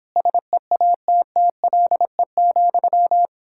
Morse code, heard as a keyed tone, about 29 wpm, 700 Hz. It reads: SEATTLE,